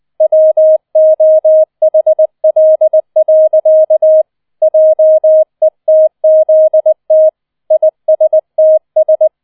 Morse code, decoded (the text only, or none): WOHL. JETZT ISTS